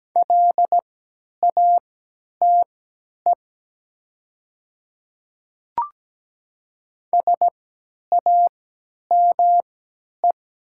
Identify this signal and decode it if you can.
Morse code, heard as a keyed tone, about 17 wpm, spaced as for 10 wpm, 700 Hz. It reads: LATE SAME
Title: 3x Character Spacing - Top 500 Words - 17wpm